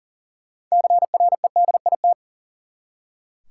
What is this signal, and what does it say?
Morse code, heard as a keyed tone, about 40 wpm, 700 Hz. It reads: CREDIT
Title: Six Letter Words - 40wpm